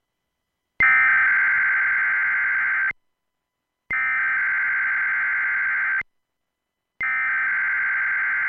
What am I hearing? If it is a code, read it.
Emergency Alert System (SAME) tones (header bursts), heard as harsh buzzing bursts. It is heard from 0.8 to 7.6 s.